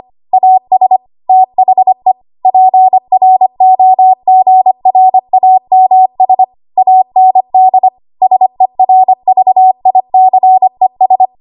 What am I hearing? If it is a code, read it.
Morse code, heard as a keyed tone, about 25 wpm, 750 Hz. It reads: AS THE PROGRAMS AND SERVICES
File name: Archive / 25 WPM / 25WPM